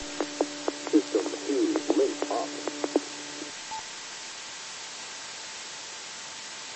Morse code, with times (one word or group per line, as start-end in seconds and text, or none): none